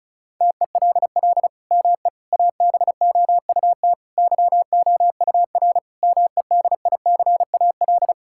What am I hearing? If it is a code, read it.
Morse code, heard as a keyed tone, about 34 wpm, 700 Hz. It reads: TELL ME ABOUT YOUR MEDICAL
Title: Sentences from Top 1000 Words - Review Part 5 - 35wpm